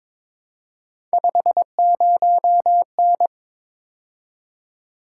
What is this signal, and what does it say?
Morse code, heard as a keyed tone, about 23 wpm, 700 Hz. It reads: 50N
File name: practice / 22wpm / Three Letter-Number Combinations - 22wpm